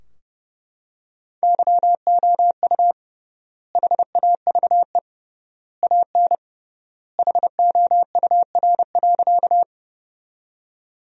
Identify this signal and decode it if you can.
Morse code, heard as a keyed tone, about 30 wpm, 700 Hz. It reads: YOU HAVE AN HOUR.
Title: 3x Word-Spacing - Sentences from Top 400 Words - 30wpm